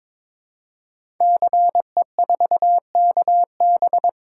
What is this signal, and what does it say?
Morse code, heard as a keyed tone, about 22 wpm, 700 Hz. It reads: CE4KB
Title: Code Groups - 5 Letter:Numbers - 22wpm